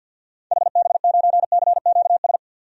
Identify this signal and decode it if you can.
Morse code, heard as a keyed tone, about 50 wpm, 700 Hz. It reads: HB9CQS
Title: DXCC-100 Call Signs - 6 Characters - 50wpm